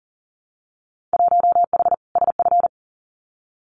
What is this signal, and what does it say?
Morse code, heard as a keyed tone, about 39 wpm, 700 Hz. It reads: 1H SF
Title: Winter Field Day Exchanges 40wpm